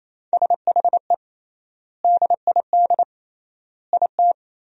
Morse code, heard as a keyed tone, about 27 wpm, 700 Hz. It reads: SHE DID IT
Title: 3x Word-Spacing - Sentences from Top 400 Words - 28wpm